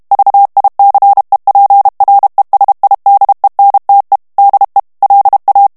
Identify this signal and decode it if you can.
Morse code, heard as a keyed tone, about 32 wpm, 800 Hz. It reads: VICEPRESIDENTE DE LA
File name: morse mp3 / ESPAGNOL TEXTE EN CLAIR 1 VITESSE 30 MOTS MINUTES